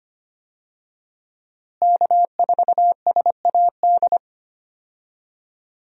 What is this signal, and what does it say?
Morse code, heard as a keyed tone, about 25 wpm, 700 Hz. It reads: K4SAD
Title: USA Call Signs - 5 Characters - Repeated 3x - 25wpm